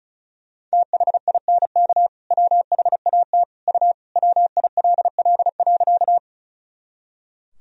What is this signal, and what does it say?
Morse code, heard as a keyed tone, about 36 wpm, 700 Hz. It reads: THINK WHAT U WILL.
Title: Sentences from Top 200 Words and Abbreviations 35wpm